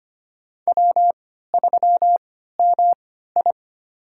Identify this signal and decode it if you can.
Morse code, heard as a keyed tone, about 25 wpm, spaced as for 14 wpm, 700 Hz. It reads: W3MI
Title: USA Call Signs - 4 Characters - 3x Character Spacing - 25wpm